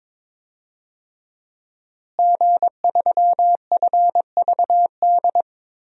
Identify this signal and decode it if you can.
Morse code, heard as a keyed tone, about 22 wpm, 700 Hz. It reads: G3FVD
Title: DXCC-100 Call Signs - 5 Characters - Repeated 3x - 22wpm